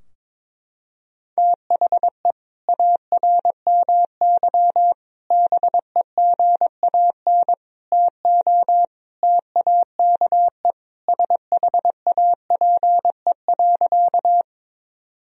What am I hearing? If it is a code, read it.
Morse code, heard as a keyed tone, about 22 wpm, 700 Hz. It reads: THE ARMY BEGAN TO TAKE SHAPE.
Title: Sentences from Top 1000 Words - Review Part 3 - 22wpm